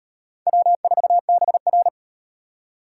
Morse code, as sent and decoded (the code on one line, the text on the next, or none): .-- ....- -... .-.
W4BR